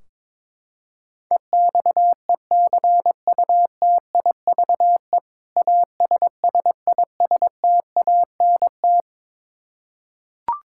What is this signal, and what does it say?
Morse code, heard as a keyed tone, about 21 wpm, 700 Hz. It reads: EXECUTIVE ASSISTANT
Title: Practice QSO - Exchange 3 - More Info - 22wpm